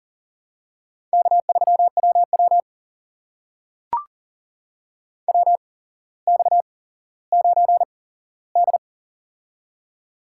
Morse code, heard as a keyed tone, about 42 wpm, spaced as for 17 wpm, 700 Hz. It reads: K3WW WX9D
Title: USA Call Signs - 4 Characters - 8x Character Spacing - 40wpm